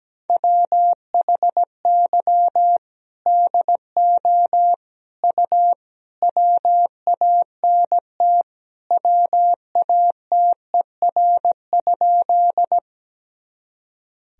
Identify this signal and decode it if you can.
Morse code, heard as a keyed tone, about 17 wpm, 700 Hz. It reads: WHY DO U WANT WATER?
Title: Sentences from Top 200 Words and Abbreviations 17wpm